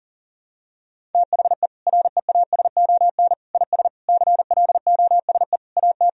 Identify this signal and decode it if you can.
Morse code, heard as a keyed tone, about 40 wpm, 700 Hz. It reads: THE REASON IS CLOSE AT
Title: Sentences from Top 1000 Words - Review Part 1 - 40wpm